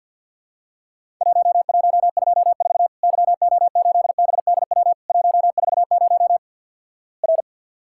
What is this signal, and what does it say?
Morse code, heard as a keyed tone, about 50 wpm, 700 Hz. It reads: 1124 YO8BDW 140 R
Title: Worked All Europe DX Contest - QTC Exchange with Repeats - 50wpm